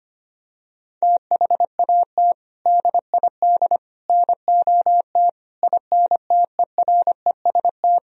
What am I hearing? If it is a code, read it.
Morse code, heard as a keyed tone, about 25 wpm, 700 Hz. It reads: THAT DID NOT INTEREST
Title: Sentences from Top 1000 Words - Review Part 2 - 25wpm